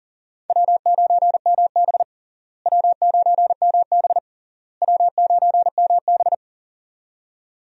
Morse code, heard as a keyed tone, about 40 wpm, 700 Hz. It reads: W9MB W9MB W9MB
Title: USA Call Signs - 4 Characters - Repeated 3x - 40wpm